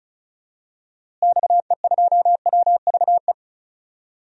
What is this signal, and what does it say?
Morse code, heard as a keyed tone, about 34 wpm, 700 Hz. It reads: XE2WVE
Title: DXCC-100 Call Signs - 6 Characters - Repeated 2x - 35wpm